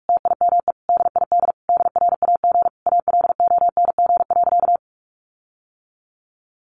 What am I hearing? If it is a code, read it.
Morse code, heard as a keyed tone, about 44 wpm, 700 Hz. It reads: TIME DID DRAG ALONG.
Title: Sentences from Top 1500 Words 45wpm